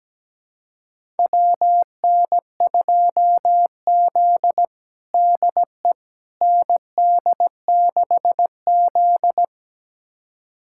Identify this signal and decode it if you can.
Morse code, heard as a keyed tone, about 17 wpm, 700 Hz. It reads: WN2Z DE ND6Z
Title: USA Callsign DE Callsign - 4 Characters - 17wpm